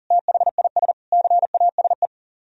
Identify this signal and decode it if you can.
Morse code, heard as a keyed tone, about 51 wpm, 700 Hz. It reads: THIS CASE